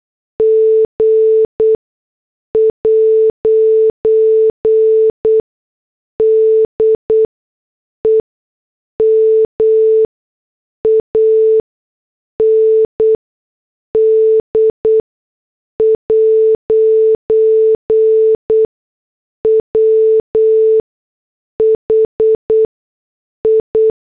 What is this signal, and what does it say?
Morse code, heard as a keyed tone, about 8 wpm, spaced as for 6 wpm, 440 Hz. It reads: G' DEMAND' WHI